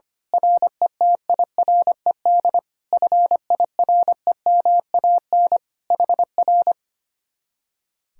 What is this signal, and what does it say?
Morse code, heard as a keyed tone, about 24 wpm, 700 Hz. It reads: RETIRED FIREMAN HR
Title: Speed-Racing Common QSO Phrases - 25wpm